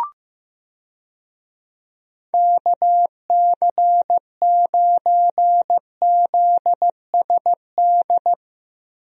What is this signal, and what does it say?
Morse code, heard as a keyed tone, about 15 wpm, 700 Hz. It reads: KC9ZSD
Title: USA Call Signs - 6 Characters - Repeated 3x - 15wpm